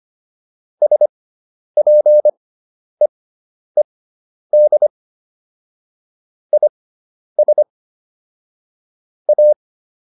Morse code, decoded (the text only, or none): SPEED IS A